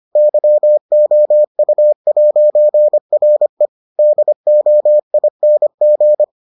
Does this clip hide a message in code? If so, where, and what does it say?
Yes: Morse code from 0.2 to 6.2 s, heard as a keyed tone, about 25 wpm, 600 Hz. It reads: YOU'RE DOING